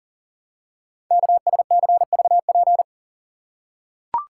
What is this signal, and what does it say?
Morse code, heard as a keyed tone, about 40 wpm, 700 Hz. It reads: KSCVP
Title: Code Groups - 5 Letters - 40wpm